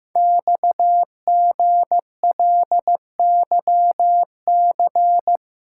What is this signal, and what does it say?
Morse code, heard as a keyed tone, about 15 wpm, 700 Hz. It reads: XGLYC
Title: Code Groups - 5 Letters - 15wpm